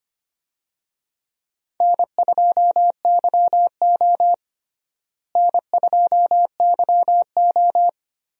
Morse code, heard as a keyed tone, about 25 wpm, 700 Hz. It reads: N2YO N2YO
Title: USA Call Signs - 4 Characters - Repeated 3x - 25wpm